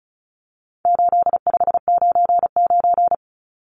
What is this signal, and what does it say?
Morse code, heard as a keyed tone, about 34 wpm, 700 Hz. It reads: 8599
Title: Numbers - 4-Digit 35wpm